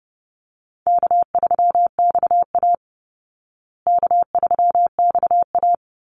Morse code, heard as a keyed tone, about 30 wpm, 700 Hz. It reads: K3XA K3XA
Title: Call Signs Repeated 3x - 30wpm